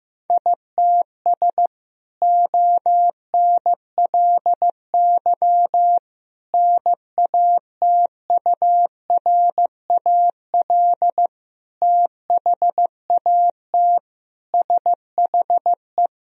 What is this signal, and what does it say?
Morse code, heard as a keyed tone, about 15 wpm, 700 Hz. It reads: ITS ONLY NATURAL THAT SHE
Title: Sentences from Top 1000 Words - Review Part 4 - 15wpm